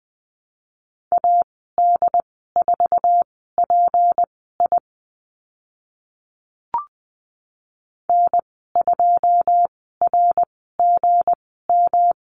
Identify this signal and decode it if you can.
Morse code, heard as a keyed tone, about 20 wpm, spaced as for 15 wpm, 700 Hz. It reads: AD4PI N2RGM
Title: 193 Lesson 31g - Letter v - Call Signs 20wpm